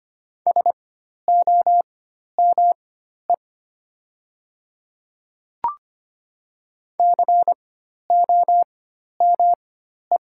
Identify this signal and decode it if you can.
Morse code, heard as a keyed tone, about 25 wpm, spaced as for 12 wpm, 700 Hz. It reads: SOME COME
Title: 4x Character Spacing - Top 100 Words - 25wpm